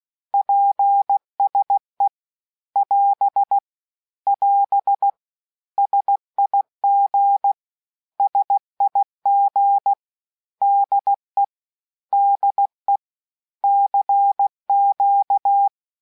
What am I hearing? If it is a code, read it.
Morse code, heard as a keyed tone, about 16 wpm, 800 Hz. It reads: PSE <AS> <AS> SIG SIG DE DE CQ
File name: cours-du-lundi files / cours-1580-abreviations-mpm-15